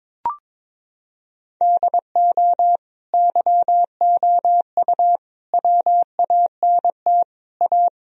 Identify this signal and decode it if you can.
Morse code, heard as a keyed tone, about 21 wpm, 700 Hz. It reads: DO YOU WANT A
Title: Sentences from Top 1000 Words - Review Part 2 - 22wpm